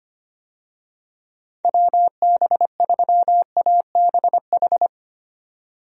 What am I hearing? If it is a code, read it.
Morse code, heard as a keyed tone, about 25 wpm, 700 Hz. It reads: WB3ABH
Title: USA Call Signs - 6 Characters - 25wpm